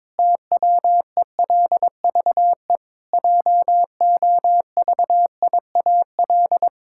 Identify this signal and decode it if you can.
Morse code, heard as a keyed tone, about 22 wpm, 700 Hz. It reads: TWELVE JOVIAL
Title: Pangram Challenge v2 - 22wpm